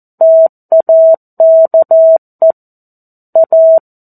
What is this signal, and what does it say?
Morse code, heard as a keyed tone, about 14 wpm, 650 Hz. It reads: TAKE A
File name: 02-Chapter-02 14 WPM